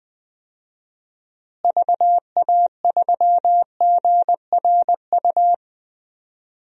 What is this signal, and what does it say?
Morse code, heard as a keyed tone, about 21 wpm, 700 Hz. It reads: VA3GRU
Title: DXCC-100 Call Signs - 6 Characters - Repeated 2x - 20wpm